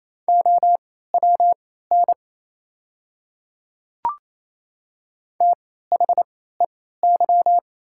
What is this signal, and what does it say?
Morse code, heard as a keyed tone, about 28 wpm, spaced as for 16 wpm, 690 Hz. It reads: OWN THEY
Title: 3x Character Spacing - Top 300 Words - 28wpm